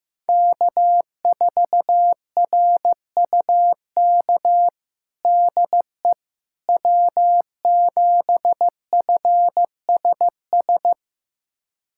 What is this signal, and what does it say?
Morse code, heard as a keyed tone, about 15 wpm, 700 Hz. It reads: K4RUK DE W7FSS
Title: USA Callsign DE Callsign - 5 Characters - 15wpm